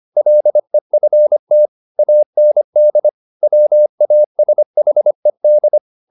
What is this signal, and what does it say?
Morse code, heard as a keyed tone, about 25 wpm, 600 Hz. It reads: LEFT AND WASHED